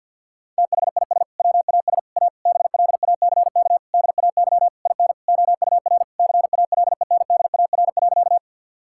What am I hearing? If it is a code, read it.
Morse code, heard as a keyed tone, about 50 wpm, 700 Hz. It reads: THIS WAS A BLACK DAY IN OUR CALENDAR.
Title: Sentences from Top 1600 Words 50wpm